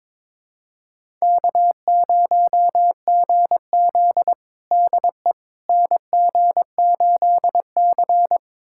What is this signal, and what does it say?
Morse code, heard as a keyed tone, about 22 wpm, 700 Hz. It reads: K0GZ DE NG8C
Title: USA Callsign DE Callsign - 4 Characters - 22wpm